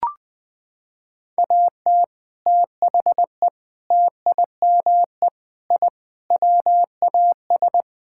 Morse code, heard as a keyed tone, about 21 wpm, 700 Hz. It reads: AT THE TIME I WAS
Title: Sentences with 3-Word Phrases - 20wpm